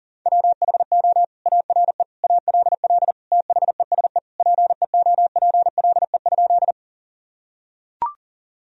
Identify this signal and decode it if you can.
Morse code, heard as a keyed tone, about 37 wpm, 700 Hz. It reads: WHO ARE ALL THESE PEOPLE?
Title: Sentences from Top 1000 Words - Review Part 1 - 40wpm